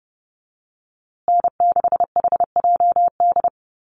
Morse code, heard as a keyed tone, about 30 wpm, 700 Hz. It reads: N6HJD
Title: USA Call Signs Repeated 3x 30wpm